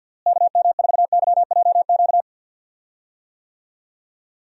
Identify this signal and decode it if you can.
Morse code, heard as a keyed tone, about 52 wpm, 700 Hz. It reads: KM4YJQ
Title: USA Call Signs - 6 Characters - Repeated 3x - 50wpm